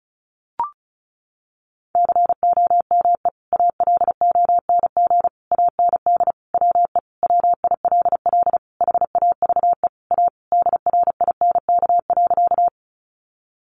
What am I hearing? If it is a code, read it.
Morse code, heard as a keyed tone, about 34 wpm, 700 Hz. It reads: COME ALONG AND WE WILL HAVE A DRINK.
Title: Sentences from Top 1500 Words 35wpm